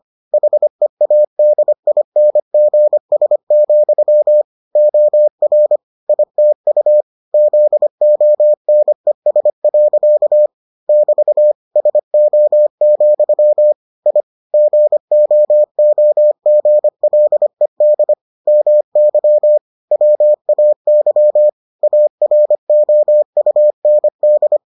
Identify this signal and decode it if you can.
Morse code, heard as a keyed tone, about 25 wpm, 600 Hz. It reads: HEADINGS, OR ITU ZONES. <BT> SO, I GOOGLED MY WAY AROUND